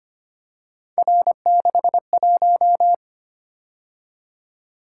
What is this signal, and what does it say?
Morse code, heard as a keyed tone, about 25 wpm, 700 Hz. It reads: R61
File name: Three Letter-Number Combinations - 25wpm